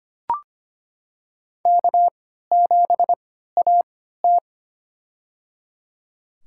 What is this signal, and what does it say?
Morse code, heard as a keyed tone, about 25 wpm, spaced as for 14 wpm, 700 Hz. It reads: K7AT
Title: USA Call Signs - 4 Characters - 3x Character Spacing - 25wpm